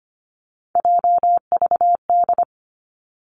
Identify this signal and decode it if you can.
Morse code, heard as a keyed tone, about 25 wpm, 700 Hz. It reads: JVD
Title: Three Challenging Letter-Number Combinations 25wpm